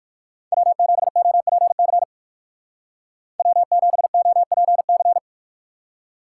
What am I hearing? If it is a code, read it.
Morse code, heard as a keyed tone, about 59 wpm, 700 Hz. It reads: W7OPC W7OPC